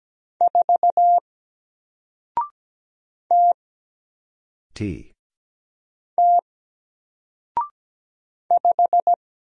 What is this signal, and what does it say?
Morse code, heard as a keyed tone, about 17 wpm, 700 Hz. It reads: 4 T T 5
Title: Single Letter-Number Alt 17wpm